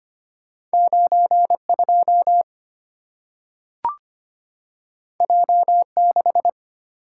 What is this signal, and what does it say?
Morse code, heard as a keyed tone, about 25 wpm, 690 Hz. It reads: 92 J6
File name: Two Letter-Number Combinations - 25wpm